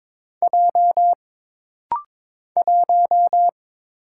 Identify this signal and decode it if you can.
Morse code, heard as a keyed tone, about 22 wpm, 700 Hz. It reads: J 1
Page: This rapid-fire timing is geared to push your ICR (Instant Character Recognition) skills.